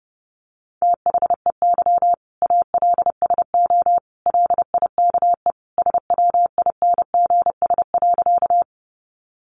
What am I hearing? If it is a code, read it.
Morse code, heard as a keyed tone, about 30 wpm, 700 Hz. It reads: THEY ALSO LIKE SWINGS.
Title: Easy Phrases 30wpm